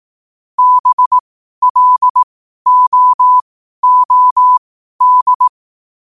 Morse code, heard as a keyed tone, about 18 wpm, spaced as for 12 wpm, 1000 Hz. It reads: BLOOD
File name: qotdfiles / 12wpm